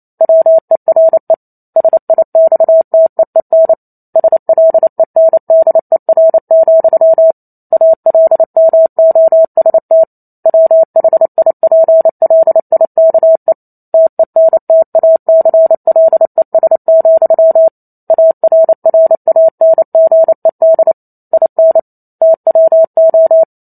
Morse code, heard as a keyed tone, about 28 wpm, 650 Hz. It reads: WERE SIXTEEN SLENDER, ALMOST WHIPLIKE TENTACLES, ARRANGED IN TWO
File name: Chapter 2 - 28 WPM